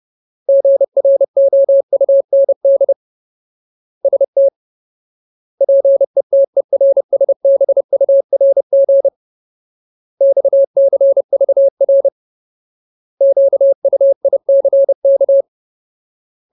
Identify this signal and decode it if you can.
Morse code, heard as a keyed tone, about 30 wpm, 550 Hz. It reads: GROUND ST PETERSBURG XCVR QUICK
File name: Ultimate Word List / 30WPM Ultimate